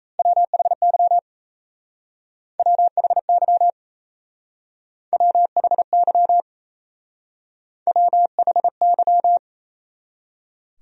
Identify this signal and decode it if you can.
Morse code, heard as a keyed tone, about 35 wpm, 700 Hz. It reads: WHY WHY WHY WHY